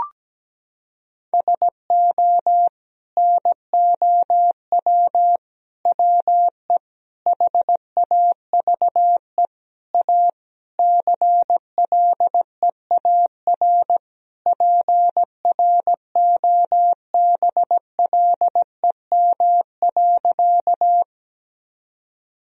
Morse code, heard as a keyed tone, about 17 wpm, 700 Hz. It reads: SO NOW WE HAVE A CLEAR PROBLEM.
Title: Sentences from Top 1000 Words - Review Part 2 - 17wpm